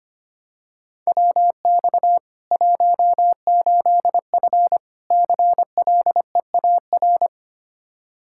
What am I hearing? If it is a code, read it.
Morse code, heard as a keyed tone, about 25 wpm, 700 Hz. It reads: WX 18F CLEAR
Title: Practice QSO - Exchange 2 - Info - 25wpm